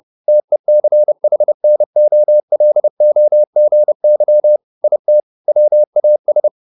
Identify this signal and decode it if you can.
Morse code, heard as a keyed tone, about 30 wpm, 600 Hz. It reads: TECHNOLOGY IT WAS